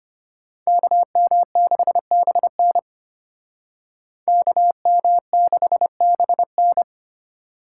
Morse code, heard as a keyed tone, about 30 wpm, 700 Hz. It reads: KM6BN KM6BN
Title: Speed Racing - USA 2x2 Call Signs 25wpm